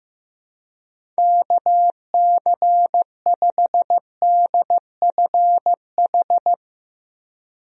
Morse code, heard as a keyed tone, about 15 wpm, 700 Hz. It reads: KC5DFH
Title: USA Call Signs - 6 Characters - Repeated 3x - 15wpm